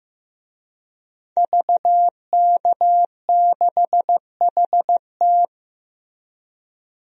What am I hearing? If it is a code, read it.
Morse code, heard as a keyed tone, about 15 wpm, 700 Hz. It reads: VK6HT
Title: DXCC-100 Call Signs - 5 Characters - Repeated 3x - 15wpm